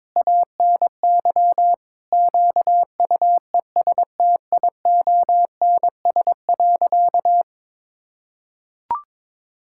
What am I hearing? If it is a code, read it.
Morse code, heard as a keyed tone, about 22 wpm, 700 Hz. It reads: ANY QUESTIONS.
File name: Sentences with 7-Word Phrases - 22wpm